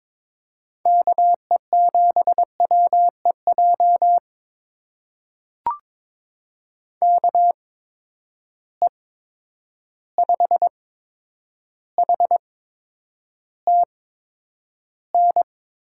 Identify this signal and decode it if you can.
Morse code, heard as a keyed tone, about 22 wpm, spaced as for 9 wpm, 700 Hz. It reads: KE7WEJ KE5HTN
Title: USA Call Signs - 6 Characters - 8x Character Spacing - 22wpm